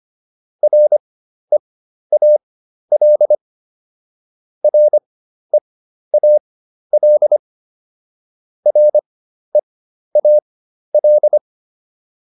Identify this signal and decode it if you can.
Morse code, heard as a keyed tone, about 25 wpm, spaced as for 12 wpm, 600 Hz. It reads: REAL REAL REAL